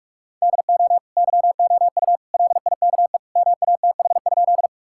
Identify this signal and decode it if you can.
Morse code, heard as a keyed tone, about 43 wpm, 700 Hz. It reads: DO YOU LIKE MATH?
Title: Sentences from Top 1000 Words - Review Part 6 - 45wpm